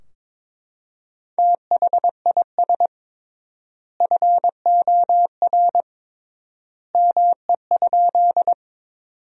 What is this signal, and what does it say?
Morse code, heard as a keyed tone, about 22 wpm, 700 Hz. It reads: THIS FOR ME?
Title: 3x Word-Spacing - Sentences from Top 200 Words - 22wpm